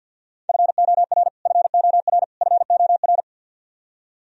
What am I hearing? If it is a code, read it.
Morse code, heard as a keyed tone, about 47 wpm, 700 Hz. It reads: FOR FOR FOR
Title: Top 100 Words Repeated 50wpm